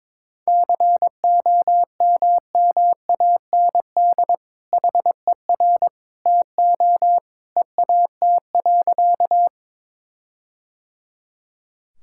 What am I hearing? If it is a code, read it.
Morse code, heard as a keyed tone, about 22 wpm, 700 Hz. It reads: COMMAND HER TO EAT.
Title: Sentences from Top 1600 Words 22wpm